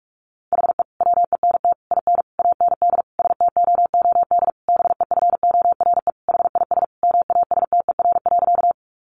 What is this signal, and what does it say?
Morse code, heard as a keyed tone, about 44 wpm, 700 Hz. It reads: HE WENT IN AND STOOD BEFORE HIS MASTER.
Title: Sentences from Top 1300 Words 45wpm